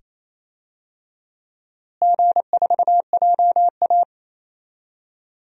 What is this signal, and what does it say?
Morse code, heard as a keyed tone, about 28 wpm, 700 Hz. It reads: G4JA
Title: DXCC-100 Call Signs - 4 Characters - Repeated 2x - 28wpm